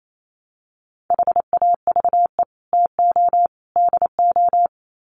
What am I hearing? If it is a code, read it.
Morse code, heard as a keyed tone, about 28 wpm, 700 Hz. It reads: HAVE TO DO
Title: Sets of 3 Words 28wpm